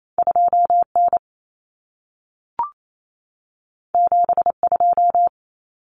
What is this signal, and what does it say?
Morse code, heard as a keyed tone, about 28 wpm, 700 Hz. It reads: 2N 72
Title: Common Cut Numbers - 2-Digit 28wpm